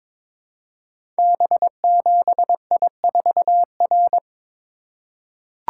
Morse code, heard as a keyed tone, about 21 wpm, 700 Hz. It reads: B7I4R
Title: Code Groups - 5 Letter:Numbers - 22wpm